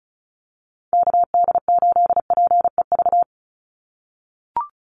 Morse code, heard as a keyed tone, about 34 wpm, 700 Hz. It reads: KD8PEV
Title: USA Call Signs 35wpm